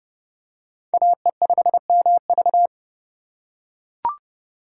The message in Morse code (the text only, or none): AE5MV